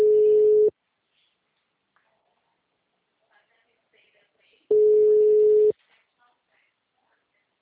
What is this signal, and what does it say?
Ringback tone (Europe, 425 Hz), with speech.
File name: out- -2064